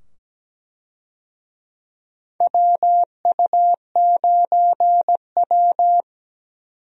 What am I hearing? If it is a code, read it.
Morse code, heard as a keyed tone, about 17 wpm, 700 Hz. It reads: WU9W
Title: USA Call Signs - 4 Characters - 17wpm